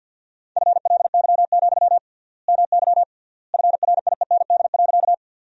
Morse code, heard as a keyed tone, about 50 wpm, 700 Hz. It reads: FLY, MY FRIEND.
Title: 3x Word-Spacing - Sentences from Top 500 Words - 50wpm